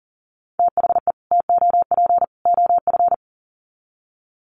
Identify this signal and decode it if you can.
Morse code, heard as a keyed tone, about 40 wpm, 700 Hz. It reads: THE TOP OF
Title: Sets of 3 Words 40wpm